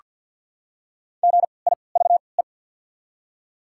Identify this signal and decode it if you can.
Morse code, heard as a keyed tone, about 48 wpm, 700 Hz. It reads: G I V E
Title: 3x Character Spacing - Top 200 Words - 50wpm